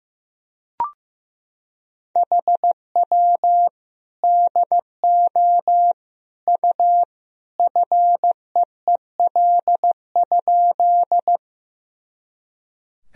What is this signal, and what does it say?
Morse code, heard as a keyed tone, about 15 wpm, 700 Hz. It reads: HW DO U FEEL?
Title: Sentences from Top 400 Words and Abbreviations 15wpm